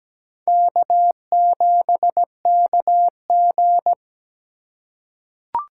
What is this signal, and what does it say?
Morse code, heard as a keyed tone, about 17 wpm, 700 Hz. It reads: K7KG
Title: USA Call Signs - 4 Characters - 2x Character Spacing - 17wpm